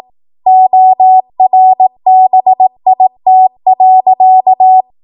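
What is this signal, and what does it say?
Morse code, heard as a keyed tone, about 18 wpm, 750 Hz. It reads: ORBIT.
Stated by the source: ARRL 18 wpm Morse code practice archive transmission.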